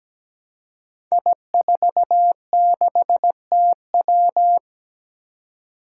Morse code, heard as a keyed tone, about 17 wpm, 700 Hz. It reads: I46TW
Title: Five Challenging Letter-Number Combinations 17wpm